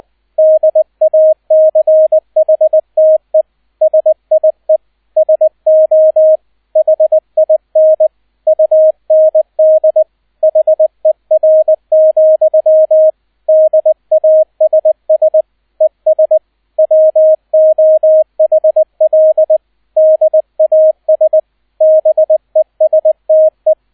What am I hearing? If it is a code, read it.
Morse code, heard as a keyed tone, about 20 wpm, 630 Hz, with radio noise. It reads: DACHTE SIE SO HIN UND HER, DASS ES WOHL DAS BESTE
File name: 002 Zweiter Streich 20wpm